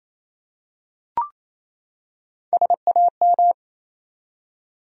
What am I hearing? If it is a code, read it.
Morse code, heard as a keyed tone, about 28 wpm, 700 Hz. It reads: SAM
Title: USA Common Names - Max 3 Letters - 28wpm